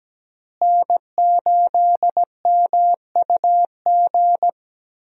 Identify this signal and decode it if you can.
Morse code, heard as a keyed tone, about 17 wpm, 700 Hz. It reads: N8MUG
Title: USA Call Signs - 5 Characters - Repeated 3x - 17wpm